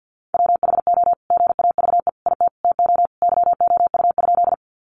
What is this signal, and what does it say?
Morse code, heard as a keyed tone, about 50 wpm, 700 Hz. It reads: WHO GAVE IT TO YOU?
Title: Sentences from Top 500 Words 50wpm